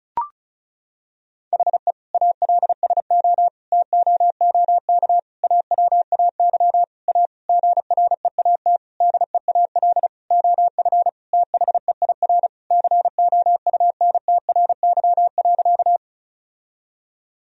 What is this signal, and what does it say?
Morse code, heard as a keyed tone, about 36 wpm, 700 Hz. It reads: HE ALSO TOOK AWAY A GREAT DEAL OF THEIR COUNTRY.
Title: Sentences from Top 1000 Words - Review Part 1 - 35wpm